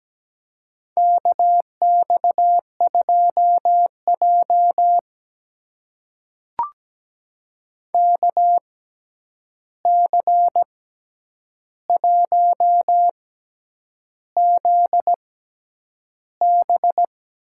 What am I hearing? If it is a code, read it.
Morse code, heard as a keyed tone, about 17 wpm, spaced as for 8 wpm, 700 Hz. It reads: KX2J KC1ZB
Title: USA Call Signs - 6x Character Spacing - 17wpm